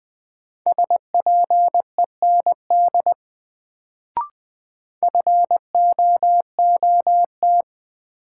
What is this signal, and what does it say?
Morse code, heard as a keyed tone, about 20 wpm, 700 Hz. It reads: SPEND FOOT
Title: Sets of 5 Words - New Word Drill - 20wpm